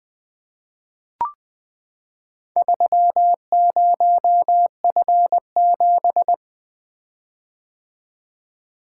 Morse code, heard as a keyed tone, about 20 wpm, 700 Hz. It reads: 30F7
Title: Four Letter-Number Combinations - 20wpm